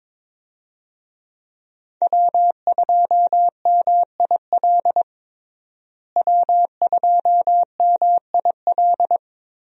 Morse code, heard as a keyed tone, about 22 wpm, 700 Hz. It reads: W2MIL W2MIL
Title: USA Call Signs - 5 Characters - Repeated 2x - 22wpm